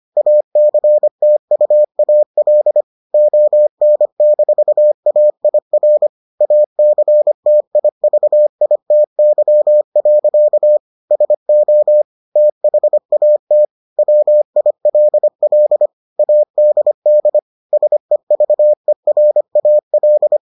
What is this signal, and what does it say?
Morse code, heard as a keyed tone, about 25 wpm, 600 Hz. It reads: ACTUAL ON-AIR ACTIVITY. SO THAT WILL ADD SEVERAL